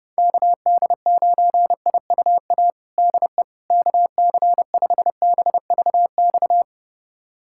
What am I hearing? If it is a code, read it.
Morse code, heard as a keyed tone, about 30 wpm, 700 Hz. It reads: KD9IUA DE KC5BVX
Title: USA Callsign DE Callsign 30wpm